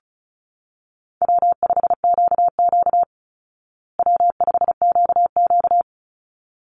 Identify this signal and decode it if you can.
Morse code, heard as a keyed tone, about 34 wpm, 700 Hz. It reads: W5QQ W5QQ
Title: Call Signs Repeated 3x - 35wpm